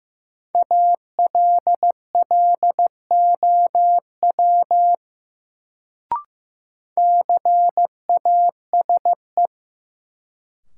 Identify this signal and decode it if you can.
Morse code, heard as a keyed tone, about 15 wpm, 700 Hz. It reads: ALLOW CASE
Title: Sentences from Top 700 Words - New Word Drill - 15wpm